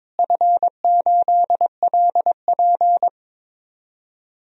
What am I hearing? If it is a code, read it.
Morse code, heard as a keyed tone, about 21 wpm, 700 Hz. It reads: F8LP
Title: DXCC-100 Call Signs - 4 Characters - Repeated 2x - 22wpm